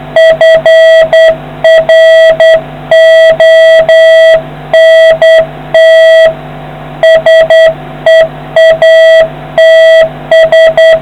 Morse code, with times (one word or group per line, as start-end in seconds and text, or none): 0.2-6.3 FRONT
7.0-11.0 SEATS